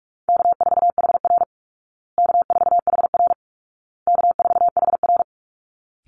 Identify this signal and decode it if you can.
Morse code, heard as a keyed tone, about 44 wpm, 700 Hz. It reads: K4HR K4HR K4HR
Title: Call Signs Repeated 2x - 45wpm